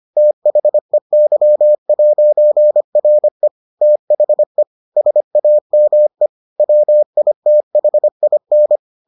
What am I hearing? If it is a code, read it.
Morse code, heard as a keyed tone, about 25 wpm, 600 Hz. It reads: THEY'RE THE SAME WITHIN